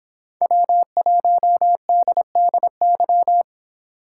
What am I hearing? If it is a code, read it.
Morse code, heard as a keyed tone, about 26 wpm, 700 Hz. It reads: W1DDY
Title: Speed Racing - USA 1x3 Call Signs 22wpm